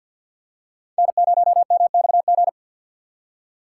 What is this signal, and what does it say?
Morse code, heard as a keyed tone, about 46 wpm, 690 Hz. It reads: N0MXG
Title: USA Call Signs - 4x Character Spacing - 50wpm